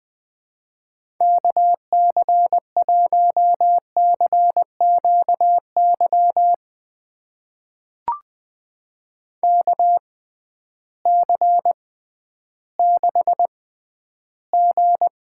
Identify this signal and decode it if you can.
Morse code, heard as a keyed tone, about 20 wpm, spaced as for 12 wpm, 700 Hz. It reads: KC1CQY KC6G
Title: USA Call Signs - 6x Character Spacing - 20wpm